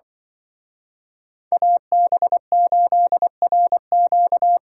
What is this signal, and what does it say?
Morse code, heard as a keyed tone, about 24 wpm, 700 Hz. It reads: AB8RQ
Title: Speed Racing - USA Call Signs 20wpm